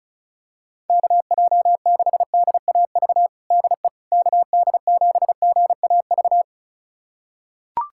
Morse code, heard as a keyed tone, about 34 wpm, 700 Hz. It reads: KJ6DAV DE KD7GAV
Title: USA Callsign DE Callsign - 6 Characters - 35wpm